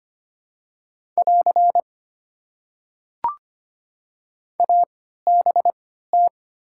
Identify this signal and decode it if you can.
Morse code, heard as a keyed tone, about 25 wpm, spaced as for 14 wpm, 700 Hz. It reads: <AR> ABT
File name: Top 90 QSO Elements - 3x Character Spacing - 25wpm